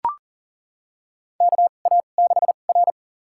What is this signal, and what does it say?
Morse code, heard as a keyed tone, about 40 wpm, spaced as for 29 wpm, 700 Hz. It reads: KA6R
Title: USA Call Signs - 4 Characters - 2x Character Spacing - 40wpm